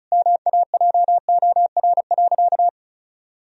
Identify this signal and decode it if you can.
Morse code, heard as a keyed tone, about 36 wpm, 700 Hz. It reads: MAJOR.
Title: Sentences from Top 1000 Words - Review Part 4 - 35wpm